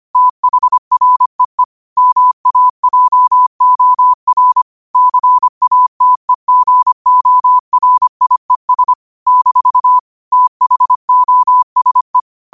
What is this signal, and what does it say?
Morse code, heard as a keyed tone, about 25 wpm, 1000 Hz. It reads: THREE MAJOR CATEGORIES - THOSE
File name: 25wpm